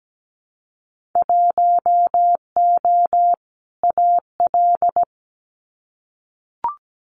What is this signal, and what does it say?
Morse code, heard as a keyed tone, about 17 wpm, 700 Hz. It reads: 1O AL
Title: Winter Field Day Exchanges Encore 17wpm